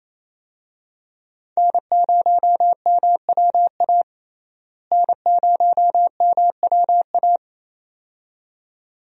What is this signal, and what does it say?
Morse code, heard as a keyed tone, about 28 wpm, 700 Hz. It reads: N0MWA N0MWA
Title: USA Call Signs - 5 Characters - Repeated 2x - 28wpm